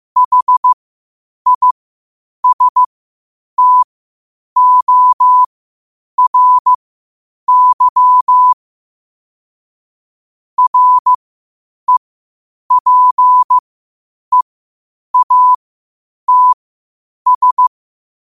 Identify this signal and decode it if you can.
Morse code, heard as a keyed tone, about 15 wpm, spaced as for 8 wpm, 1000 Hz. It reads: HISTORY REPEATS
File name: qotdfiles / 8wpm